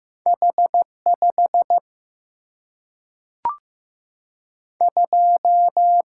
Morse code, heard as a keyed tone, about 15 wpm, 700 Hz. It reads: H 5 2
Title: Single-Letter Single-Number 15wpm